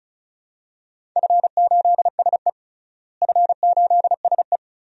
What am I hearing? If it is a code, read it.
Morse code, heard as a keyed tone, about 36 wpm, 700 Hz. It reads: F8SE F8SE
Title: DXCC-100 Call Signs - 4 Characters - Repeated 2x - 35wpm